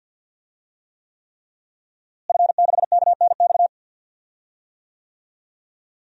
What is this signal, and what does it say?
Morse code, heard as a keyed tone, about 52 wpm, 700 Hz. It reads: F6KNX
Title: DXCC-100 Call Signs - 5 Characters - Repeated 3x - 50wpm